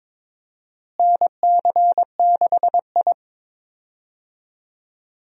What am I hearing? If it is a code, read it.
Morse code, heard as a keyed tone, about 21 wpm, 700 Hz. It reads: NC6I
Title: USA Call Signs - 4 Characters - Repeated 2x - 22wpm